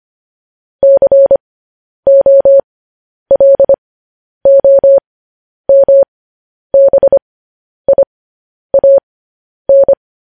Morse code, heard as a keyed tone, about 25 wpm, spaced as for 10 wpm, 550 Hz. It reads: COLOMBIAN